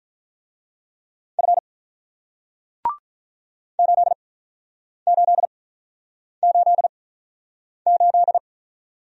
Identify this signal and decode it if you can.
Morse code, heard as a keyed tone, about 48 wpm, spaced as for 9 wpm, 700 Hz. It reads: F 8888